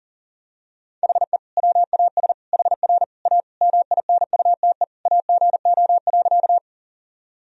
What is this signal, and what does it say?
Morse code, heard as a keyed tone, about 37 wpm, 700 Hz. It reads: HE WAS HR A MINUTE AGO.
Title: Sentences from Top 500 Words and Abbreviations 40wpm